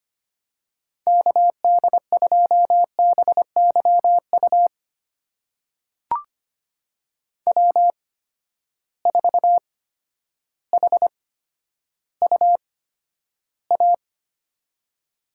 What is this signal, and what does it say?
Morse code, heard as a keyed tone, about 25 wpm, spaced as for 11 wpm, 700 Hz. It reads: KD2BYU W4HUA
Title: USA Call Signs - 8x Character Spacing - 25wpm